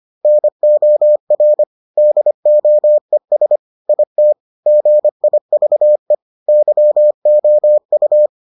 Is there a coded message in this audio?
Yes: Morse code from 0.2 to 8.3 s, heard as a keyed tone, about 25 wpm, 600 Hz. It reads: NOR DOES IT GIVE YOU